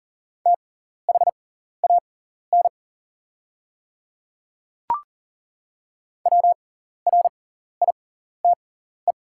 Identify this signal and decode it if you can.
Morse code, heard as a keyed tone, about 40 wpm, spaced as for 14 wpm, 690 Hz. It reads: THAN WRITE
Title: 6x Character Spacing - Top 100 Words - 40wpm